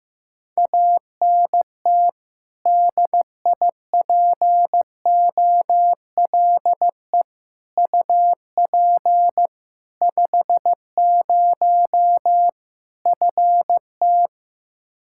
Morse code, heard as a keyed tone, about 15 wpm, 700 Hz. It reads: ANT DIPOLE UP 50 FT
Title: Practice QSO - Exchange 2 - Info - 15wpm